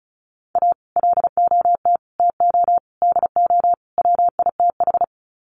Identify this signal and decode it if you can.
Morse code, heard as a keyed tone, about 34 wpm, 700 Hz. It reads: A LOT TO DO WITH
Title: Sets of 5 Words 35wpm